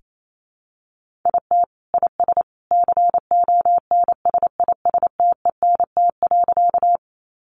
Morse code, heard as a keyed tone, about 28 wpm, 700 Hz. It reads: IT IS CONSISTENT.
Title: Sentences from Top 1500 Words 28wpm